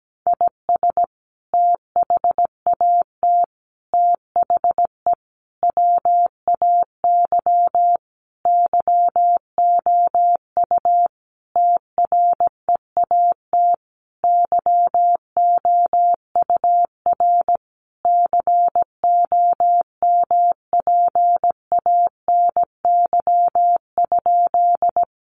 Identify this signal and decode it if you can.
Morse code, heard as a keyed tone, about 17 wpm, 700 Hz. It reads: IS THAT THE WAY YOU TREAT YOUR COMPANY?
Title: Sentences from Top 1300 Words 17wpm